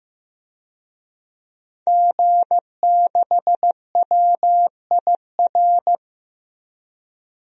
Morse code, heard as a keyed tone, about 15 wpm, 700 Hz. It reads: G6WIR
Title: DXCC-100 Call Signs - 5 Characters - Repeated 2x - 15wpm